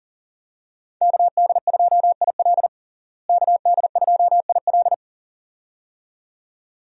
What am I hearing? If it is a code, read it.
Morse code, heard as a keyed tone, about 40 wpm, 700 Hz. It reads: KD2IL KD2IL
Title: USA Call Signs - 5 Characters - Repeated 2x - 40wpm